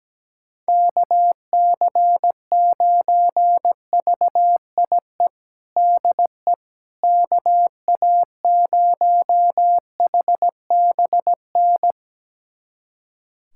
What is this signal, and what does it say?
Morse code, heard as a keyed tone, about 17 wpm, 700 Hz. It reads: KC9VIE DE KA0HBN
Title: USA Callsign DE Callsign - 6 Characters - 17wpm